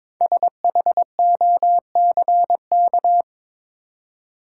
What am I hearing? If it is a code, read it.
Morse code, heard as a keyed tone, about 22 wpm, 700 Hz. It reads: SHOCK
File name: 5 Letter Words - First 3 Letters Given - 22wpm